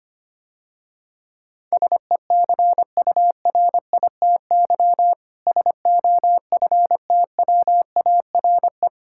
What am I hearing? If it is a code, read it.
Morse code, heard as a keyed tone, about 26 wpm, 700 Hz. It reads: SECURITY SOFTWARE
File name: practice / 25wpm / Difficult Words - Speed-Racing with Hints - 25wpm